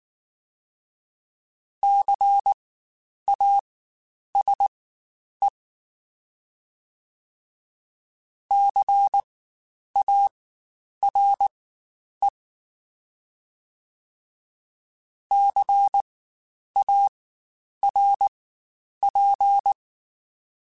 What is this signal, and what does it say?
Morse code, heard as a keyed tone, about 19 wpm, spaced as for 9 wpm, 800 Hz. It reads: CASE CARE CARP
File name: Spaced Alphas / Spaced Alphas mp3s / Cs sp @18